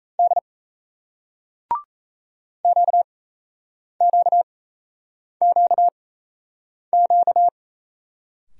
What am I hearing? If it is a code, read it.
Morse code, heard as a keyed tone, about 38 wpm, spaced as for 8 wpm, 700 Hz. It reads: D QQQQ